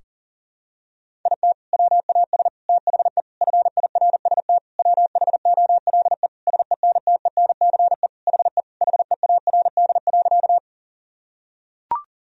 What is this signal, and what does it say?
Morse code, heard as a keyed tone, about 40 wpm, 700 Hz. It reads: IT WAS THE FIRST WHOLE SENTENCE HE HEARD.
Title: Sentences from Top 1000 Words - Review Part 1 - 40wpm